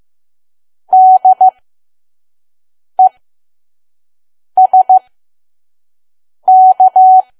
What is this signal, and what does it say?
Morse code, heard as a keyed tone, about 15 wpm, spaced as for 5 wpm, 750 Hz. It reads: DESK
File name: Archive / 5 WPM files / PM